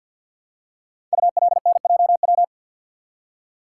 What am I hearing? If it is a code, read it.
Morse code, heard as a keyed tone, about 50 wpm, 690 Hz. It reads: ULNJW
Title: Code Groups - 5 Letters - 50wpm